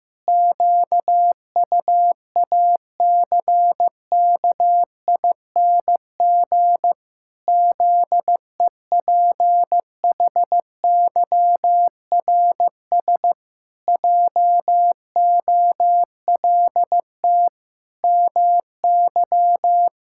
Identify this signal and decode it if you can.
Morse code, heard as a keyed tone, about 16 wpm, 700 Hz. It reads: QUACKING ZEPHYRS JOLT MY
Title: Pangram Challenge v2 - 15wpm